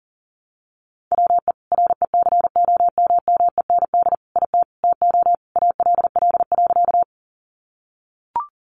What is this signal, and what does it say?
Morse code, heard as a keyed tone, about 40 wpm, 700 Hz. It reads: WE RECOMMEND IT TO ALL.
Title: Sentences from Top 1200 Words 40wpm